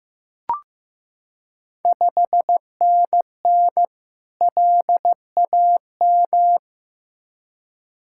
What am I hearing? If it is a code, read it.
Morse code, heard as a keyed tone, about 15 wpm, 700 Hz. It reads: 5NN LAM
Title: Ontario QSO Party Exchange 15wpm